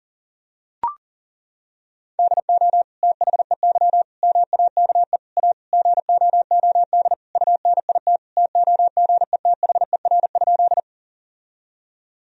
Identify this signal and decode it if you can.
Morse code, heard as a keyed tone, about 40 wpm, 700 Hz. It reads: DO THEY MAKE A GOOD UNIT TOGETHER?
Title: Sentences from Top 1000 Words - Review Part 2 - 40wpm